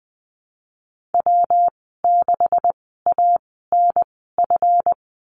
Morse code, heard as a keyed tone, about 20 wpm, spaced as for 15 wpm, 700 Hz. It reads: W6ANF
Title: 154 Lesson 25g - Letter p - Call Signs 20wpm